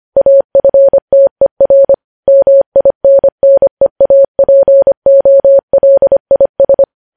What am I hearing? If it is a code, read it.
Morse code, heard as a keyed tone, about 25 wpm, 550 Hz. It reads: AFTER MINNEAPOLIS
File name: podcast / News headlines-1x-25x25